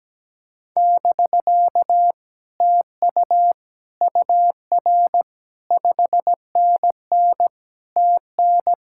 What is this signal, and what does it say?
Morse code, heard as a keyed tone, about 17 wpm, 700 Hz. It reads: <BK> TU UR 5NN TN
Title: Speed-Racing Common QSO Phrases Encore - 17wpm